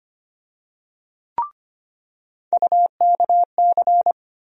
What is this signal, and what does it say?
Morse code, heard as a keyed tone, about 25 wpm, 700 Hz. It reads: UKC
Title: USA Call Sign Postfixes - 3 Letters 25wpm